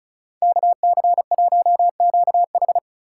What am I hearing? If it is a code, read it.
Morse code, heard as a keyed tone, about 45 wpm, 700 Hz. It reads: KC1QH